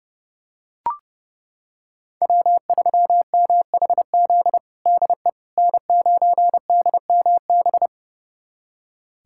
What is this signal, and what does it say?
Morse code, heard as a keyed tone, about 32 wpm, 700 Hz. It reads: W3MHZ DE N9DMB
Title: USA Callsign DE Callsign - 5 Characters - 30wpm